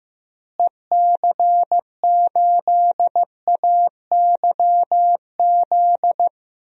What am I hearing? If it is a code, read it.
Morse code, heard as a keyed tone, about 15 wpm, 700 Hz. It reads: EC8AYZ
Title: DXCC-100 Call Signs - 6 Characters - Repeated 3x - 15wpm